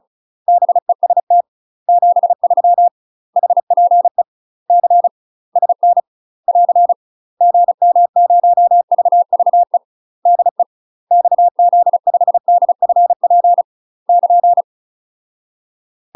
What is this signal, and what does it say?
Morse code, heard as a keyed tone, about 34 wpm, 700 Hz. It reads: BEST 73 HPE C SN <AR> GM0VVE DE XZ5DFP <KN>
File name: UK MASTER QSOs TRAINING / 35WPM / UK MASTER QSOs 0015